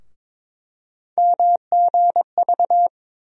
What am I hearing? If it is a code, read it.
Morse code, heard as a keyed tone, about 23 wpm, 700 Hz. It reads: MGV
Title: USA Call Sign Postfixes - 3 Letters 22wpm